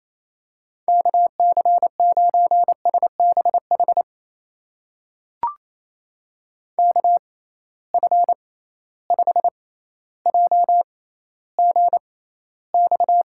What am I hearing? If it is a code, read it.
Morse code, heard as a keyed tone, about 27 wpm, spaced as for 14 wpm, 700 Hz. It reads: KC9SBH KF5JGX
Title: USA Call Signs - 6 Characters - 6x Character Spacing - 28wpm